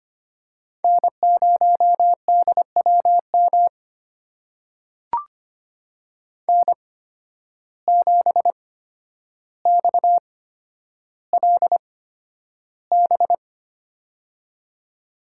Morse code, heard as a keyed tone, about 25 wpm, spaced as for 11 wpm, 700 Hz. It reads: N0DWM N7XLB
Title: USA Call Signs - 5 Characters - 8x Character Spacing - 25wpm